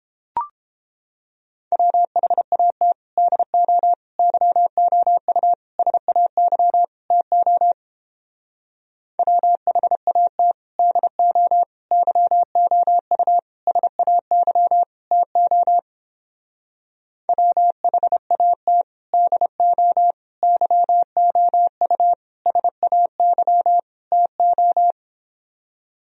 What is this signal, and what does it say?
Morse code, heard as a keyed tone, about 31 wpm, 700 Hz. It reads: WHAT DO YOU SAY TO WHAT DO YOU SAY TO WHAT DO YOU SAY TO